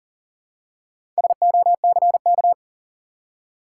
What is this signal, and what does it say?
Morse code, heard as a keyed tone, about 40 wpm, 700 Hz. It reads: SOCK
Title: Four Letter Words - 40wpm